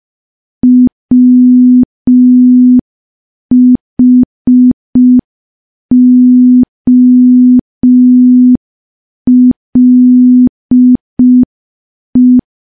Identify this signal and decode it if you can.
Morse code, heard as a keyed tone, about 5 wpm, 250 Hz. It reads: WHOLE
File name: morse / morse-training-05WPM